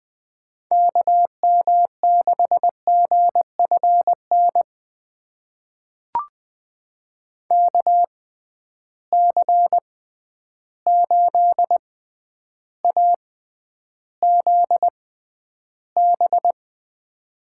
Morse code, heard as a keyed tone, about 21 wpm, spaced as for 10 wpm, 700 Hz. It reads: KM6GFN KC8AZB
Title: USA Call Signs - 6 Characters - 6x Character Spacing - 20wpm